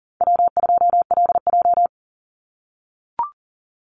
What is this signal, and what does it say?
Morse code, heard as a keyed tone, about 40 wpm, 700 Hz. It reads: W2LJ
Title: Call Signs - 40wpm